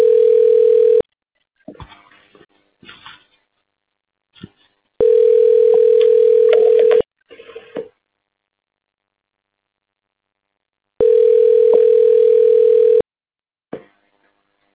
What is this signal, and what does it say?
Ringback tone (North America), 440 + 480 Hz.